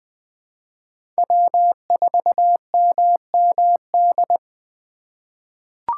Morse code, heard as a keyed tone, about 20 wpm, 700 Hz. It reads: W4MMD
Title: K1USN SST QSO Practice - 20wpm